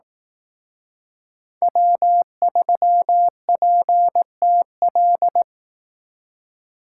Morse code, heard as a keyed tone, about 18 wpm, 700 Hz. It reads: W3PTL
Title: Speed Racing - USA 1x3 Call Signs 15wpm